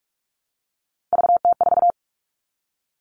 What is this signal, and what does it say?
Morse code, heard as a keyed tone, about 44 wpm, 700 Hz. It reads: VT4
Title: All Cut Numbers - 3-Digit 45wpm